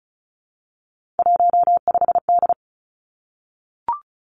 Morse code, heard as a keyed tone, about 36 wpm, 700 Hz. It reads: 15D
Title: Three Challenging Letter-Number Combinations 35wpm